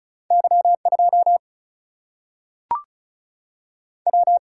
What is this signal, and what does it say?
Morse code, heard as a keyed tone, about 36 wpm, 700 Hz. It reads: Y 2 W
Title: Single-Letter Single-Number 35wpm